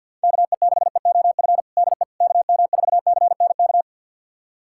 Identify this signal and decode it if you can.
Morse code, heard as a keyed tone, about 52 wpm, 700 Hz. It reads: KE6EOF DE KM4CNK
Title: USA Callsign DE Callsign - 6 Characters - 50wpm